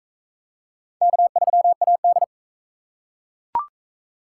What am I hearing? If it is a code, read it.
Morse code, heard as a keyed tone, about 44 wpm, 700 Hz. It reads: K3AD